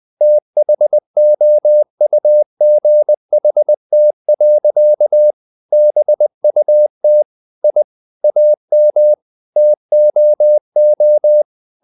Morse code, heard as a keyed tone, about 20 wpm, 600 Hz. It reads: THOUGHT. BUT I AM TOO
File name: the old man and the sea part 2 20wpm